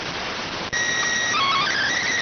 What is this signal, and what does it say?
SSTV (slow-scan television), PD 290.